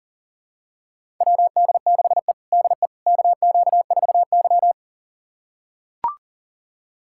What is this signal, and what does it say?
Morse code, heard as a keyed tone, about 40 wpm, 700 Hz. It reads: WD6E DE KQ4Y
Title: USA Callsign DE Callsign - 4 Characters - 40wpm